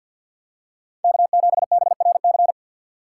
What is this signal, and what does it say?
Morse code, heard as a keyed tone, about 53 wpm, 700 Hz. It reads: K7BRC
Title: USA Call Signs - 5 Characters - 4x Character Spacing - 50wpm